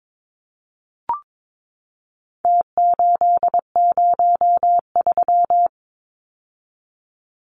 Morse code, heard as a keyed tone, about 22 wpm, 700 Hz. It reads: T803
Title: Numbers - Contest Style - 4-Digit 22wpm